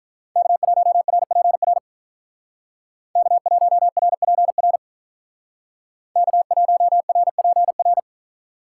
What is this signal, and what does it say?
Morse code, heard as a keyed tone, about 42 wpm, 700 Hz. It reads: K1RPR K1RPR K1RPR
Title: Speed Racing - USA Call Signs 35wpm